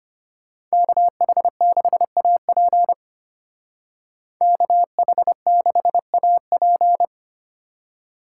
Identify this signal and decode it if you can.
Morse code, heard as a keyed tone, about 31 wpm, 700 Hz. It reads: KH6AP KH6AP
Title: Speed Racing - USA 2x2 Call Signs 25wpm